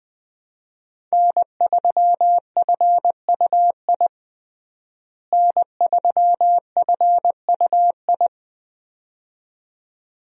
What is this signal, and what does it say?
Morse code, heard as a keyed tone, about 20 wpm, 700 Hz. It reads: N3FUI N3FUI
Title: USA Call Signs - 5 Characters - Repeated 2x - 20wpm